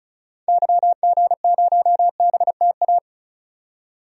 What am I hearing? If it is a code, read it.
Morse code, heard as a keyed tone, about 36 wpm, 700 Hz. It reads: YG0BTA
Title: DXCC-100 Call Signs - 6 Characters - 35wpm